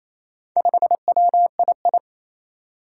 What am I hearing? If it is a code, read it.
Morse code, heard as a keyed tone, about 28 wpm, 700 Hz. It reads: 5WII
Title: Four Challenging Letter-Number Combinations 28wpm